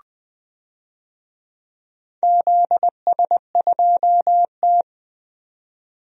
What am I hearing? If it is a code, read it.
Morse code, heard as a keyed tone, about 21 wpm, 700 Hz. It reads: ZS2T
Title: DXCC-100 Call Signs - 4 Characters - 20wpm